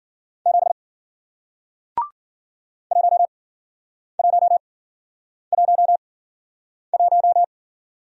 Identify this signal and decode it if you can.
Morse code, heard as a keyed tone, about 52 wpm, spaced as for 9 wpm, 700 Hz. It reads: 7 1111